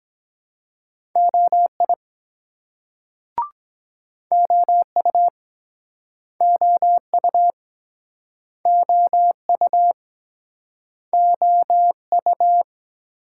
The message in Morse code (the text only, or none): OI OU OU OU OU